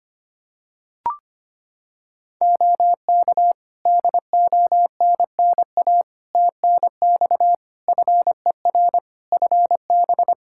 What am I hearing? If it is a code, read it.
Morse code, heard as a keyed tone, about 25 wpm, 700 Hz. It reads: OK DONNA TNX FER FB
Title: Practice QSO - Exchange 4 - Ending - 25wpm